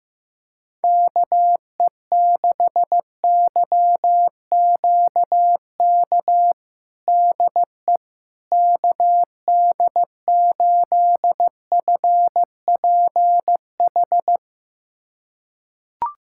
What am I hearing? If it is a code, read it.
Morse code, heard as a keyed tone, about 15 wpm, 700 Hz. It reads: KE6YQK DE KD8FPH
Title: USA Callsign DE Callsign 15wpm